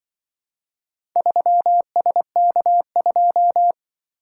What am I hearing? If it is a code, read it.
Morse code, heard as a keyed tone, about 25 wpm, 700 Hz. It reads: 3SK2
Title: Speed-Racing - Four Letter-Number Combinations - 20wpm